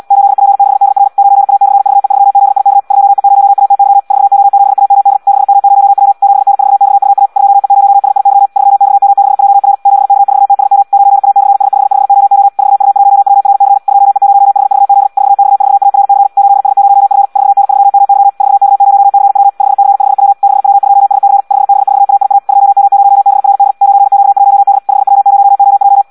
Morse code, heard as a keyed tone, about 74 wpm, 800 Hz. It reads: 8P6DR 9A/VE3ZIK 2E0ATY 4O7AMD 6M0W 7N4XUA 2E0SIY 3XM6JR 7K4TSM 9A/S55OO 4U1WRC 2E0HVZ 4X4ARC 8S0F 4N3MY 4K0CW 5C5W 6W2SC 5B4AIA 2M0BRD 9Q1D 5W0OJ
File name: Audio practica CW / qrq-tx-78 2